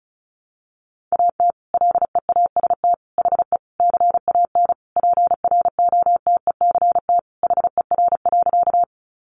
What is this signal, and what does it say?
Morse code, heard as a keyed tone, about 34 wpm, 700 Hz. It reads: AT LEAST HE CAN PROTECT HER.
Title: Sentences from Top 900 Words 35wpm